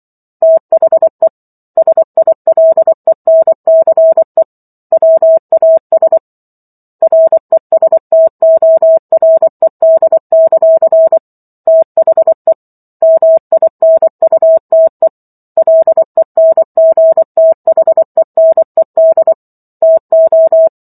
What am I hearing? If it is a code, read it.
Morse code, heard as a keyed tone, about 24 wpm, 650 Hz. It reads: THE SILENCE WAS RESTORED; THE MINUTE LENGTHENED TO
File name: Chapter 15 - 24 WPM